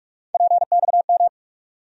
Morse code, heard as a keyed tone, about 46 wpm, 700 Hz. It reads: PXM
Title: USA Call Sign Postfixes - 3 Letters 45wpm